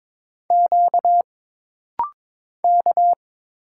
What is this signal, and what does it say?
Morse code, heard as a keyed tone, about 22 wpm, 700 Hz. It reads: Q K
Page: This rapid-fire timing is geared to push your ICR (Instant Character Recognition) skills.